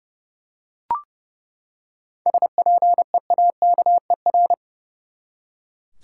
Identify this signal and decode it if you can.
Morse code, heard as a keyed tone, about 30 wpm, 700 Hz. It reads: SPEAKER
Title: Seven Letter Words - 30wpm